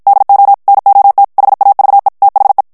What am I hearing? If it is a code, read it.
Morse code, heard as a keyed tone, about 56 wpm, 800 Hz. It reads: DO NOT HAVE THE
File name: morse mp3 / ANGLAIS texte clair 1 vitesse 50 mots minute